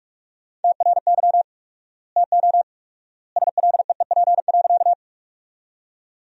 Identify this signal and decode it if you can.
Morse code, heard as a keyed tone, about 47 wpm, 700 Hz. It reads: TRY TO SLEEP.
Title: 4x Word-Spacing - Sentences from Top 500 Words - 45wpm